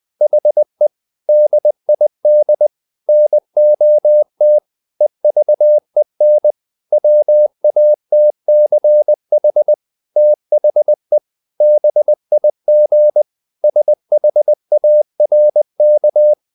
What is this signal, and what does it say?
Morse code, heard as a keyed tone, about 20 wpm, 600 Hz. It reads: HE DID NOT EVEN WATCH THE BIG SHARK
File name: the old man and the sea part 2 20wpm